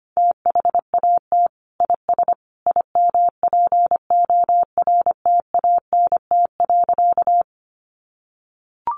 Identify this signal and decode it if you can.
Morse code, heard as a keyed tone, about 25 wpm, 700 Hz. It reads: THAT IS IMPORTANT.
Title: Sentences from Top 600 Words 25wpm